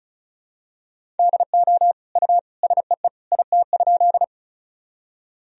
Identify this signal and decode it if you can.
Morse code, heard as a keyed tone, about 34 wpm, 700 Hz. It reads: DO U SEE IT?
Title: Sentences from Top 100 Words and Abbreviations 35wpm